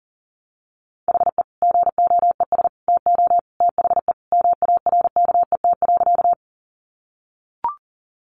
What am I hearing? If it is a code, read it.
Morse code, heard as a keyed tone, about 40 wpm, 700 Hz. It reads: HE GOES TO THE MARKET.
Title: Additional Easy Phrases 40wpm